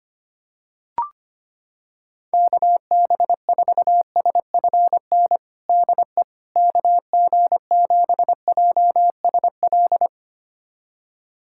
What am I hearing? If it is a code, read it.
Morse code, heard as a keyed tone, about 25 wpm, 700 Hz. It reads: KB4SFN DE KG7JSL
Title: USA Callsign DE Callsign - 6 Characters - 25wpm